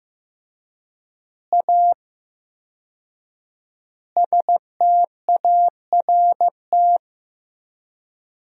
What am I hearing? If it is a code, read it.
Morse code, heard as a keyed tone, about 15 wpm, 700 Hz. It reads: A START
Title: 4x Word-Spacing - Sentences from Top 300 Words - 15wpm